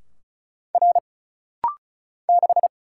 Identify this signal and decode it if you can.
Morse code, heard as a keyed tone, about 35 wpm, 700 Hz. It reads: R 6
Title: Single Letter-Number - Rapid-Fire 35wpm